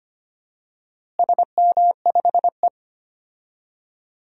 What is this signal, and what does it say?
Morse code, heard as a keyed tone, about 25 wpm, 700 Hz. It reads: SM5E
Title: DXCC-100 Call Signs - 4 Characters - Repeated 2x - 25wpm